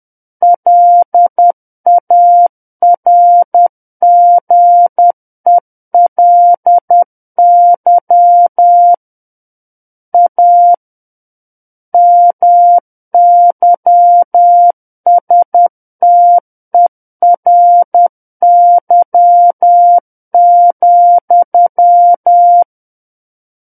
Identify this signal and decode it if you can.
Morse code, heard as a keyed tone, about 10 wpm, 700 Hz. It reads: LARGELY A MYSTERY,
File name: Book One / 01-Chapter 1 - 10 WPM